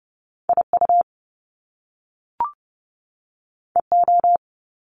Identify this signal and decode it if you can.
Morse code, heard as a keyed tone, about 30 wpm, 700 Hz. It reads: IU EO
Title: Top Two Letter Combinations in Words 30wpm